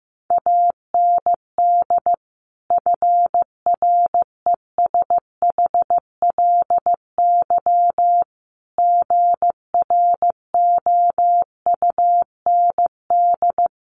Morse code, heard as a keyed tone, about 15 wpm, 700 Hz. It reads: AND FRESHLY GROUND
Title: Sets of 5 Words Encore 15wpm